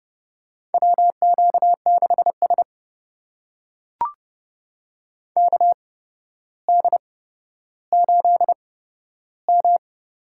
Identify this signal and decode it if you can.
Morse code, heard as a keyed tone, about 30 wpm, spaced as for 13 wpm, 700 Hz. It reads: WQ6S KD8M
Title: USA Call Signs - 4 Characters - 8x Character Spacing - 30wpm